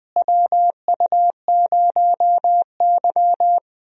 Morse code, heard as a keyed tone, about 21 wpm, 700 Hz. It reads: WU0Y
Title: USA Call Signs - 4 Characters - Repeated 2x - 20wpm